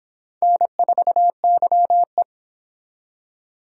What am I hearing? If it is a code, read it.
Morse code, heard as a keyed tone, about 26 wpm, 700 Hz. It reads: N4YE